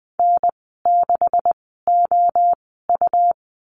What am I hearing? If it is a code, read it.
Morse code, heard as a keyed tone, about 20 wpm, spaced as for 15 wpm, 700 Hz. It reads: N6OU
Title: 179 Lesson 29g - Slash - Call Signs 20wpm